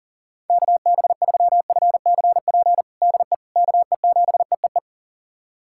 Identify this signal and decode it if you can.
Morse code, heard as a keyed tone, about 40 wpm, 700 Hz. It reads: KB3FCP DE KE7EEE
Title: USA Callsign DE Callsign 40wpm